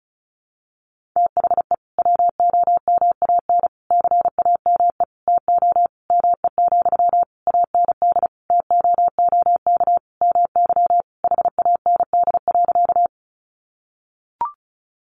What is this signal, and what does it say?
Morse code, heard as a keyed tone, about 34 wpm, 700 Hz. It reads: THE WOMAN CAME TO ME, AND TOOK MY HAND.
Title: Sentences from Top 1100 Words 35wpm